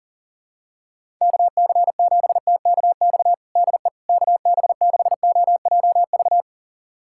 Morse code, heard as a keyed tone, about 40 wpm, 700 Hz. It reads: KC7TKX DE KB6OJV
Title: USA Callsign DE Callsign - 6 Characters - 40wpm